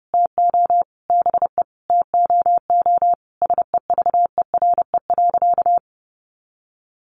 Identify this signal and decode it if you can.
Morse code, heard as a keyed tone, about 28 wpm, 700 Hz. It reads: TO BE TOO SEVERE.
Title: Sentences from Top 1400 Words 30wpm